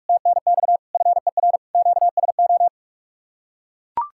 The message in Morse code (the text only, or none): TNX FER QSO